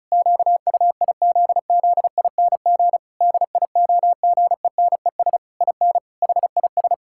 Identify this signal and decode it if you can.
Morse code, heard as a keyed tone, about 36 wpm, 700 Hz. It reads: QUIZZING DIOGENES IN HIS
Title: Pangram Challenge v2 - 35wpm